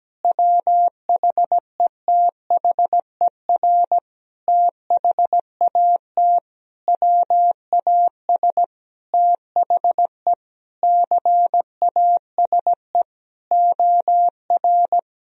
Morse code, heard as a keyed tone, about 17 wpm, 700 Hz. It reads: WHETHER THAT WAS THE CASE OR
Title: Sentences from Top 1000 Words - Review Part 3 - 17wpm